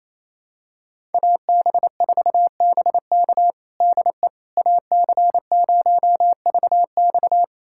Morse code, heard as a keyed tone, about 28 wpm, 700 Hz. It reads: AB4BK DE AC0VX
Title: USA Callsign DE Callsign - 5 Characters - 28wpm